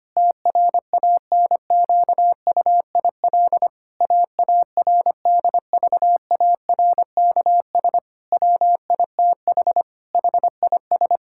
Morse code, heard as a keyed tone, about 25 wpm, 700 Hz. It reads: TRANQUIL AARDVARKS WITH HIS
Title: Pangram Challenge v2 - 25wpm